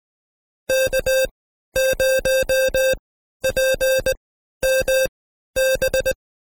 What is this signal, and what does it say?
Morse code, heard as a keyed tone, about 19 wpm, spaced as for 12 wpm, 530 Hz. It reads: K0PMB
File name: Code Quick / AudioDownld2 / Seg132-Stepper5 11WPM